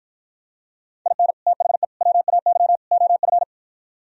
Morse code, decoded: IN THE WAY OF

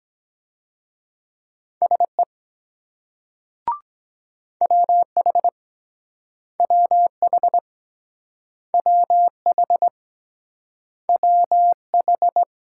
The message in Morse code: SE WH WH WH WH

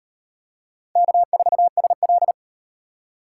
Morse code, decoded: K4SL

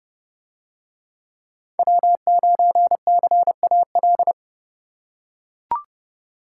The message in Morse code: W9CAL

